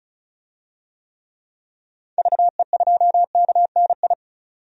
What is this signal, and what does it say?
Morse code, heard as a keyed tone, about 36 wpm, 700 Hz. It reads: VE2KNI
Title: DXCC-100 Call Signs - 6 Characters - 35wpm